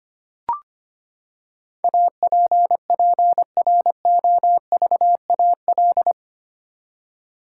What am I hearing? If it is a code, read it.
Morse code, heard as a keyed tone, about 25 wpm, 700 Hz. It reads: APPROVAL
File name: Eight Letter Words - 25wpm